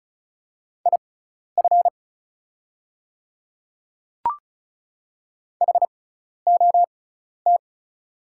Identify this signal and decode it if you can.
Morse code, heard as a keyed tone, about 36 wpm, spaced as for 12 wpm, 690 Hz. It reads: IF HOT
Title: 6x Character Spacing - Top 300 Words - 35wpm